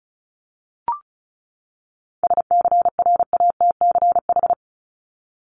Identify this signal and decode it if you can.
Morse code, heard as a keyed tone, about 34 wpm, 700 Hz. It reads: SCRATCH
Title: Top 1501-1750 Words 35wpm